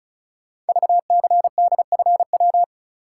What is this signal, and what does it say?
Morse code, heard as a keyed tone, about 36 wpm, 700 Hz. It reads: VCDFW
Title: Code Groups - 5 Letters - 35wpm